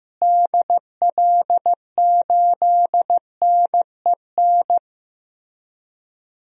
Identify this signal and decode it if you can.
Morse code, heard as a keyed tone, about 16 wpm, 700 Hz. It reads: DL8NEN
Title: DXCC-100 Call Signs - 6 Characters - Repeated 3x - 15wpm